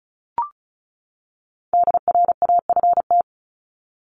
Morse code, heard as a keyed tone, about 34 wpm, 700 Hz. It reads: DRAFT
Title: Top 1751-2000 Words 35wpm